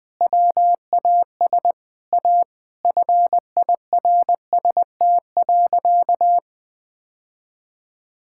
Morse code, heard as a keyed tone, about 19 wpm, 700 Hz. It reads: WAS A FIRST.
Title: Sentences with 5-Word Phrases - Spoken Beginning - 20wpm